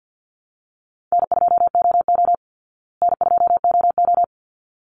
Morse code, heard as a keyed tone, about 50 wpm, 700 Hz. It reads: N2OO N2OO
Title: Easy Call Signs Repeated 2x - 50wpm